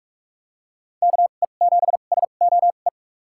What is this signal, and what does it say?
Morse code, heard as a keyed tone, about 46 wpm, spaced as for 32 wpm, 700 Hz. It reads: KE7SOE
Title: USA Call Signs - 2x Character Spacing - 45wpm